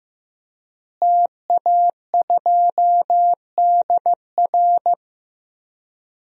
Morse code, heard as a keyed tone, about 15 wpm, 700 Hz. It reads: TA2DR
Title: DXCC-100 Call Signs - 5 Characters - Repeated 2x - 15wpm